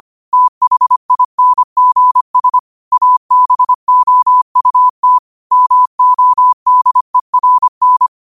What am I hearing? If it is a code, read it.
Morse code, heard as a keyed tone, about 25 wpm, 1000 Hz. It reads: THINGS ABOUT MODERN
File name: qotdfiles / 25wpm